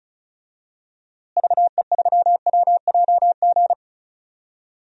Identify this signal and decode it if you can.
Morse code, heard as a keyed tone, about 36 wpm, 700 Hz. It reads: VE3WJG
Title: DXCC-100 Call Signs - 6 Characters - Repeated 2x - 35wpm